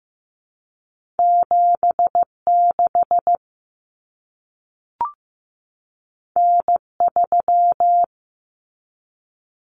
Morse code, heard as a keyed tone, about 15 wpm, 700 Hz. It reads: 76 N3
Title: Common Cut Numbers - 2-Digit 15wpm